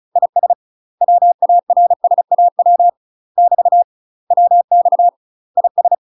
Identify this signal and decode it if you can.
Morse code, heard as a keyed tone, about 34 wpm, 700 Hz. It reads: IS WARSAW <BT> WX IS
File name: UK MASTER QSOs TRAINING / 35WPM / UK MASTER QSOs 0001